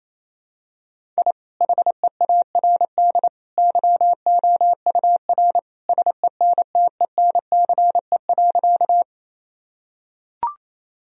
Morse code, heard as a keyed tone, about 28 wpm, 700 Hz. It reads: I HEARD YOUR SENTENCE.
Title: Sentences from Top 1000 Words - Review Part 1 - 28wpm